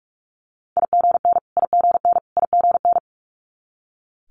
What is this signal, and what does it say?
Morse code, heard as a keyed tone, about 46 wpm, 700 Hz. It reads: IGN IGN IGN
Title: Multi-letter Phonemes Repeated 3x 45wpm